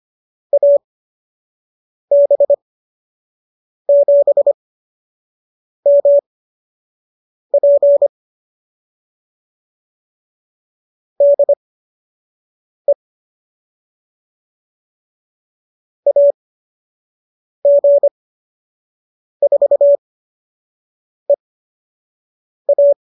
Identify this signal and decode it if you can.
Morse code, heard as a keyed tone, about 25 wpm, spaced as for 6 wpm, 590 Hz. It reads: AB7MP DE AG4EA